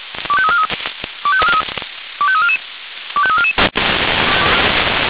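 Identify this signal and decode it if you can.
Five-tone selective call (ZVEI), heard as quick runs of five tones, tones of about 70 ms. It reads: 25343 25344 25344
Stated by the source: ZVEI five-tone selective call received on SDR.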